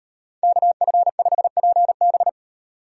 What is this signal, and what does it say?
Morse code, heard as a keyed tone, about 38 wpm, 700 Hz. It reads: KF5PB